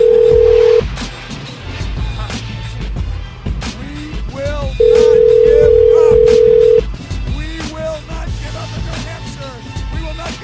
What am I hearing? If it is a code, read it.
Ringback tone (North America), 440 + 480 Hz.